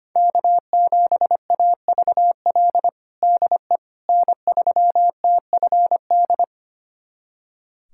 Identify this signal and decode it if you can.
Morse code, heard as a keyed tone, about 25 wpm, 700 Hz. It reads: K7AVL DE N3TFD
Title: USA Callsign DE Callsign - 5 Characters - 25wpm